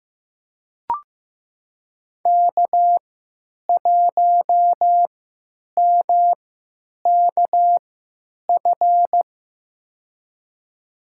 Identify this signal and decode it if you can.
Morse code, heard as a keyed tone, about 15 wpm, spaced as for 9 wpm, 700 Hz. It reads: K1MKF
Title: USA Call Signs - 5 Characters - 3x Character Spacing - 15wpm